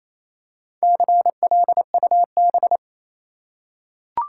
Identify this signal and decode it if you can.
Morse code, heard as a keyed tone, about 28 wpm, 700 Hz. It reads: CLUB
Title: Four Letter Words - 28wpm